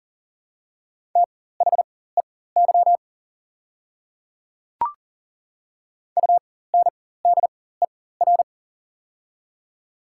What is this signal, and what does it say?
Morse code, heard as a keyed tone, about 40 wpm, spaced as for 19 wpm, 700 Hz. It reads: THEY UNDER
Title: 4x Character Spacing - Top 300 Words - 40wpm